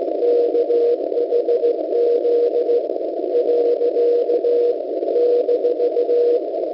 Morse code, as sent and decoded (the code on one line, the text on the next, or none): -.- .... --.. / .-.-.- / -....-
KHZ . -